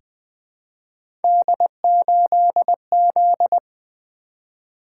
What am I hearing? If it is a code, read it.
Morse code, heard as a keyed tone, about 20 wpm, 700 Hz. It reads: D8Z
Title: Three Letter-Number Combinations - 20wpm